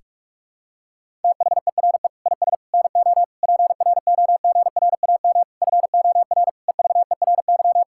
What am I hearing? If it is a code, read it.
Morse code, heard as a keyed tone, about 41 wpm, 700 Hz. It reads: THERE IS NO PROGRAM FOR EVERY
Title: Sentences from Top 1000 Words - Review Part 5 - 45wpm